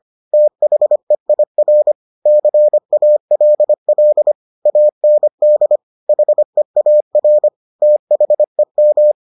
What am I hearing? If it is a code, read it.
Morse code, heard as a keyed tone, about 25 wpm, 600 Hz. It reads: THEIR CALL AND HEAR THEM